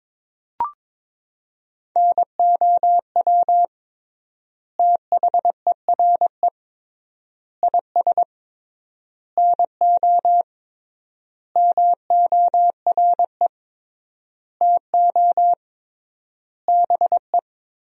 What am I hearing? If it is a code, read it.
Morse code, heard as a keyed tone, about 22 wpm, 700 Hz. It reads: NOW THERE IS NO MORE TO BE
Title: 3x Word-Spacing - Sentences from Top 100 Words - 22wpm